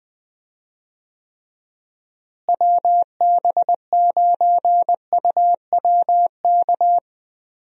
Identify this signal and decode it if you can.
Morse code, heard as a keyed tone, about 20 wpm, 700 Hz. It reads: WB9UWK
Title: USA Call Signs - 6 Characters - Repeated 2x - 20wpm